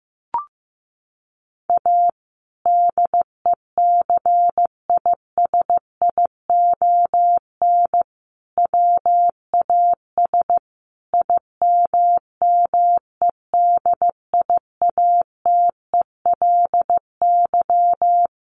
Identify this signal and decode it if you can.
Morse code, heard as a keyed tone, about 15 wpm, 700 Hz. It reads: A DECISION WAS IMMEDIATELY
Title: Sentences from Top 1100 Words 15wpm